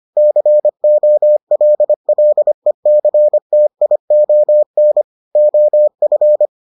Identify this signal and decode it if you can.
Morse code, heard as a keyed tone, about 25 wpm, 600 Hz. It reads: COLLECTION OF